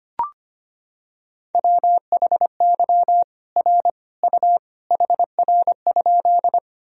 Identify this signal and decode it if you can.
Morse code, heard as a keyed tone, about 25 wpm, 700 Hz. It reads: WHY R U HR?
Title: Sentences from Top 200 Words and Abbreviations 25wpm